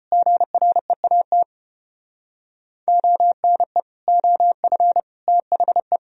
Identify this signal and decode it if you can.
Morse code, heard as a keyed tone, about 32 wpm, 700 Hz. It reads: GREAT ONE OF THE